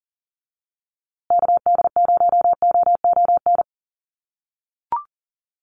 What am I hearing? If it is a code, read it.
Morse code, heard as a keyed tone, about 40 wpm, 700 Hz. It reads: KD0OON
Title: USA Call Signs Repeated 2x 40wpm